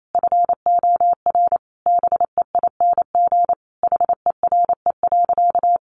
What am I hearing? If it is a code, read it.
Morse code, heard as a keyed tone, about 28 wpm, 700 Hz. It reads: FOR BEING HERE.
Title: Sentences from Top 700 Words 28wpm